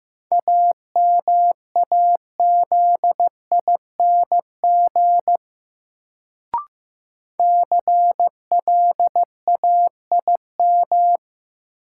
Morse code, heard as a keyed tone, about 15 wpm, 700 Hz. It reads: AMAZING CLAIM
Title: Sentences from Top 1500 Words - New Word Drill - 15wpm